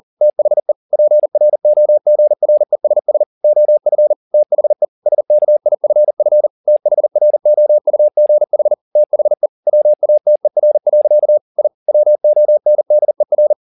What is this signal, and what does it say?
Morse code, heard as a keyed tone, about 40 wpm, 600 Hz. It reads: THE PROGRESS OF THE SKIFF THROUGH THE WATER. I WONDER